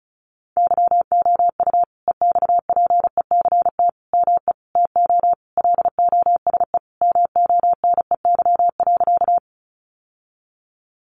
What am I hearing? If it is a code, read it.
Morse code, heard as a keyed tone, about 34 wpm, 700 Hz. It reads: YOU EXPECT ME TO LOSE MONEY.
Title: Sentences from Top 1100 Words 35wpm